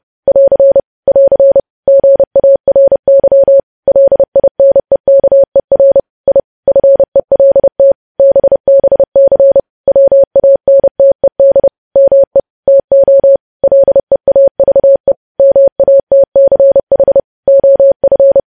Morse code, heard as a keyed tone, about 30 wpm, 550 Hz. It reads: <AR> <AR> GARY LINEKER I FELT BBC WANTED ME TO LEAVE MATCH OF
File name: podcast / News headlines-1x-30x30